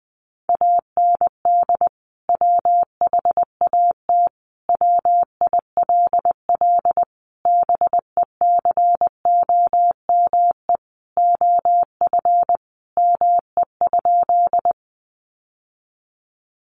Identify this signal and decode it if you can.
Morse code, heard as a keyed tone, about 20 wpm, 700 Hz. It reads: AND WHAT WILL BECOME OF ME?
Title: Sentences from Top 600 Words 20wpm